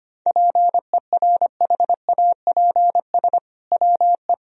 Morse code, heard as a keyed tone, about 25 wpm, 700 Hz. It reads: PERHAPS WE
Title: Sentences from Top 1000 Words - Review Part 4 - 25wpm